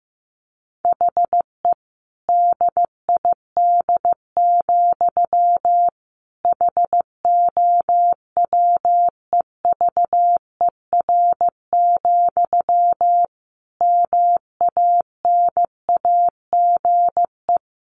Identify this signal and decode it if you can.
Morse code, heard as a keyed tone, about 15 wpm, 700 Hz. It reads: HE DID, HOWEVER, MANAGE
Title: Sentences from Top 1400 Words 15wpm